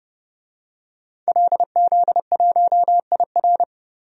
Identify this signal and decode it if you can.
Morse code, heard as a keyed tone, about 30 wpm, 700 Hz. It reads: LZ1IR
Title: DXCC-100 Call Signs - 5 Characters - Repeated 3x - 30wpm